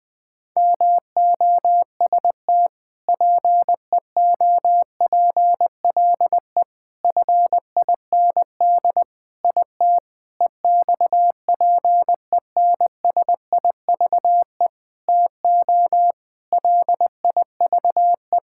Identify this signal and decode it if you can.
Morse code, heard as a keyed tone, about 20 wpm, 700 Hz. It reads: MOST PEOPLE FIND IT EXPENSIVE TO LIVE
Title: Sentences from Top 1000 Words - Review Part 6 - 20wpm